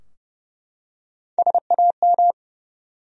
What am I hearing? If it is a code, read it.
Morse code, heard as a keyed tone, about 30 wpm, 700 Hz. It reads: SAM
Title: USA Common Names - Max 4 Letters - 30wpm